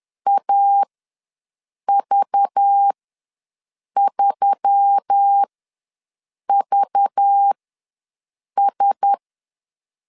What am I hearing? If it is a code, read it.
Morse code, heard as a keyed tone, about 10 wpm, spaced as for 6 wpm, 800 Hz. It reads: AV3VS
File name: LEZIONI / 5 LBVC / 5 LBV ws 10wpm 3645